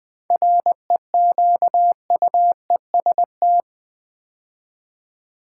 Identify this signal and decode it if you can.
Morse code, heard as a keyed tone, about 20 wpm, 690 Hz. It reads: REQUEST
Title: 7 Letter Words - First 3 Letters Given - 20wpm